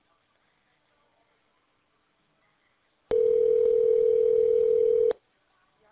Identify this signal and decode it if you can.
Ringback tone (North America), 440 + 480 Hz.